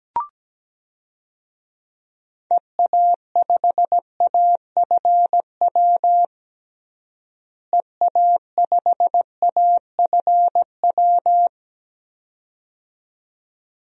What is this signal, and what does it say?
Morse code, heard as a keyed tone, about 17 wpm, 700 Hz. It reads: EA5AFW EA5AFW
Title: DXCC-100 Call Signs - 6 Characters - Repeated 2x - 17wpm